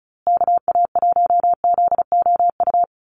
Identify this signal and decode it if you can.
Morse code, heard as a keyed tone, about 34 wpm, 700 Hz. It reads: KA1ZOU
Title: USA Call Signs Repeated 2x 35wpm